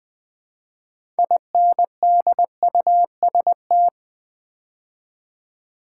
Morse code, heard as a keyed tone, about 20 wpm, 700 Hz. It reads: INDUST
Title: Build A Word - 3 Letters At A Time - 8-Letter Words - 20wpm